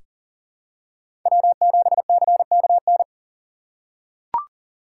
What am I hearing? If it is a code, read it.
Morse code, heard as a keyed tone, about 40 wpm, 700 Hz. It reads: W7CKN
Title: USA Call Signs - 5 Characters - 4x Character Spacing - 40wpm